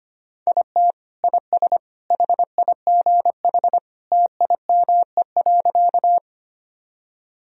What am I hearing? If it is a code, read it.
Morse code, heard as a keyed tone, about 25 wpm, 700 Hz. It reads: IT IS HIGH TIME.
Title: Sentences from Top 1000 Words - Review Part 1 - 25wpm